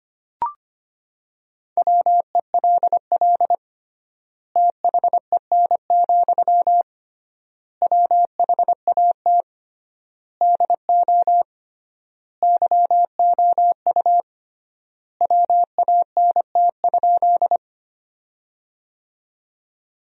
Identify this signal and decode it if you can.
Morse code, heard as a keyed tone, about 25 wpm, 700 Hz. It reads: WELL THEN, WHAT DO YOU WANT?
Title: 3x Word-Spacing - Sentences from Top 200 Words - 25wpm